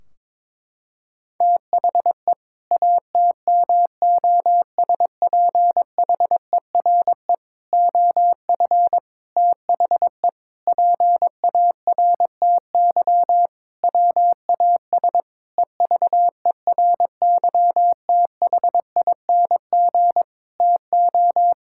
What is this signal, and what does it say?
Morse code, heard as a keyed tone, about 22 wpm, 700 Hz. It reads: THE ATMOSPHERE OF THE PARTY WAS EVERYTHING TO
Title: Sentences from Top 1600 Words 22wpm